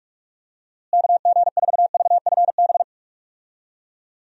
Morse code, heard as a keyed tone, about 41 wpm, 700 Hz. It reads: KG4VFB
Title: USA Call Signs - 6 Characters - Repeated 2x - 45wpm